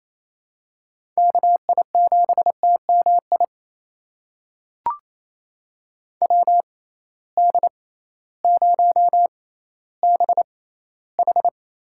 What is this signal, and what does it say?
Morse code, heard as a keyed tone, about 27 wpm, spaced as for 15 wpm, 700 Hz. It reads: KI7TMI WD0BH
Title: USA Call Signs - 6 Characters - 6x Character Spacing - 28wpm